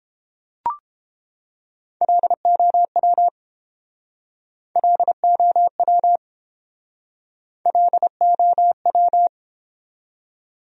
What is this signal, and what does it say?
Morse code, heard as a keyed tone, about 29 wpm, 700 Hz. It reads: LOW LOW LOW